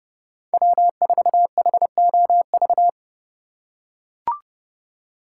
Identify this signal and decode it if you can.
Morse code, heard as a keyed tone, about 30 wpm, 700 Hz. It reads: W4HOV